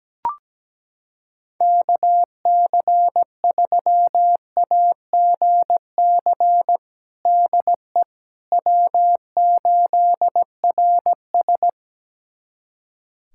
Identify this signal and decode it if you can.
Morse code, heard as a keyed tone, about 17 wpm, 700 Hz. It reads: KC3AGC DE W8RS
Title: USA Callsign DE Callsign 17wpm